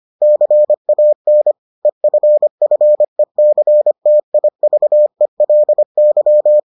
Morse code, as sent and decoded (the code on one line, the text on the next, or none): -.-. .- -. / . ..-. ..-. -.-. - .. ...- . .-.. -.--
CAN EFFECTIVELY